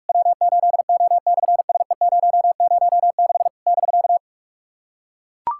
Morse code, heard as a keyed tone, about 43 wpm, 700 Hz. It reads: W8O/SE006 <BK>
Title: SOTA QSO Practice 45wpm